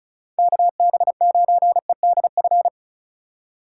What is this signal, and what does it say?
Morse code, heard as a keyed tone, about 36 wpm, 700 Hz. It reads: KB9EDF
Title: USA Call Signs - 4x Character Spacing - 35wpm